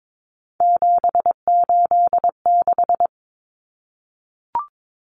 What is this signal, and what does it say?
Morse code, heard as a keyed tone, about 22 wpm, 700 Hz. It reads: 786
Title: Common Cut Numbers - 3-Digit 22wpm